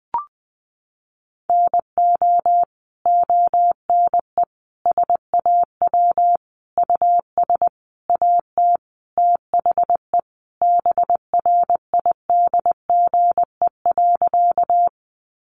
Morse code, heard as a keyed tone, about 20 wpm, 700 Hz. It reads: NO ONE SAW US AT THE BRIDGE.
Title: Sentences from Top 1200 Words 20wpm